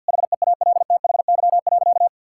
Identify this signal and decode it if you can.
Morse code, heard as a keyed tone, about 51 wpm, 700 Hz. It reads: HEALTHY.
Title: Sentences from Top 1000 Words - Review Part 5 - 50wpm